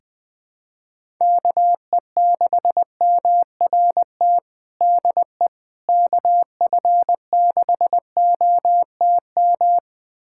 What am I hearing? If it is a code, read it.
Morse code, heard as a keyed tone, about 21 wpm, 700 Hz. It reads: KE6MRT DE KF6OTM
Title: USA Callsign DE Callsign - 6 Characters - 20wpm